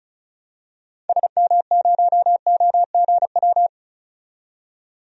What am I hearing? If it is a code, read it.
Morse code, heard as a keyed tone, about 36 wpm, 700 Hz. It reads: SM0OGW
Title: DXCC-100 Call Signs - 6 Characters - 35wpm